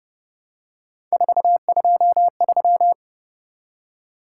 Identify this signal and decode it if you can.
Morse code, heard as a keyed tone, about 29 wpm, 700 Hz. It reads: 423
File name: Speed-Racing - Three Letter-Number Combinations - 25wpm